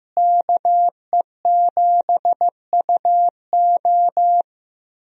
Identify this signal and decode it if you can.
Morse code, heard as a keyed tone, about 16 wpm, 700 Hz. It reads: KE7UO
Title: USA Call Signs - 5 Characters - Repeated 3x - 15wpm